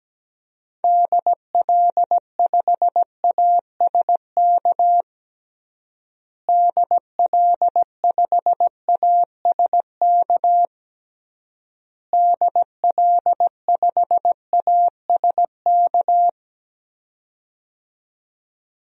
Morse code, heard as a keyed tone, about 17 wpm, 700 Hz. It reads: DL5ASK DL5ASK DL5ASK
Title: DXCC-100 Call Signs - 6 Characters - Repeated 3x - 17wpm